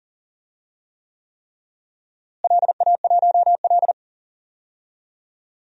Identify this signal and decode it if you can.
Morse code, heard as a keyed tone, about 37 wpm, 700 Hz. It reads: LA1L
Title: DXCC-100 Call Signs - 4 Characters - Repeated 3x - 40wpm